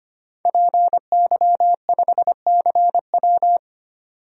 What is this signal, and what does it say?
Morse code, heard as a keyed tone, about 25 wpm, 690 Hz. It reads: PY5CW
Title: DXCC-100 Call Signs - 5 Characters - 25wpm